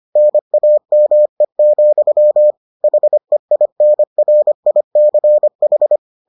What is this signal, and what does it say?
Morse code, heard as a keyed tone, about 25 wpm, 600 Hz. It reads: NAME, HEINRICH